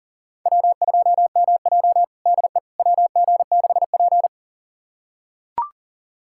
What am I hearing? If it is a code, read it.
Morse code, heard as a keyed tone, about 40 wpm, 700 Hz. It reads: W2MJ DE WG6P
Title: USA Callsign DE Callsign - 4 Characters - 40wpm